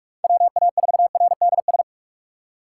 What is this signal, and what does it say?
Morse code, heard as a keyed tone, about 44 wpm, 700 Hz. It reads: WA4RDS
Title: USA Call Signs - 6 Characters - 8x Character Spacing - 45wpm